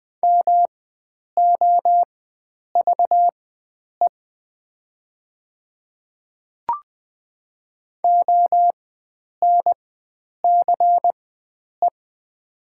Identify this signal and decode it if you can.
Morse code, heard as a keyed tone, about 20 wpm, spaced as for 9 wpm, 700 Hz. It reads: MOVE ONCE
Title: 4x Character Spacing - Top 500 Words - 20wpm